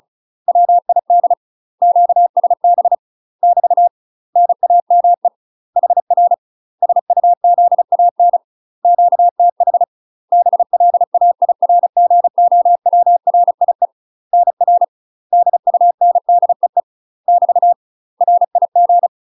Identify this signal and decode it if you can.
Morse code, heard as a keyed tone, about 36 wpm, 700 Hz. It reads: WID QSB <BT> NAME HR SUZAN QTH BLAIRGOWRIE NR DUNDEE <BT> RIG
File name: UK MASTER QSOs TRAINING / 35WPM / UK MASTER QSOs 0010